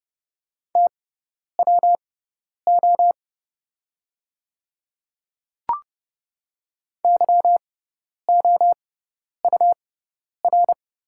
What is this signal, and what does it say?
Morse code, heard as a keyed tone, about 30 wpm, spaced as for 10 wpm, 700 Hz. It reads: TWO YOUR
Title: 6x Character Spacing - Top 100 Words - 30wpm